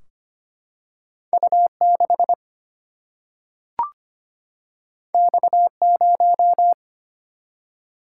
Morse code, heard as a keyed tone, about 25 wpm, 700 Hz. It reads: U6 X0
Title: Single-Letter Single-Number 25wpm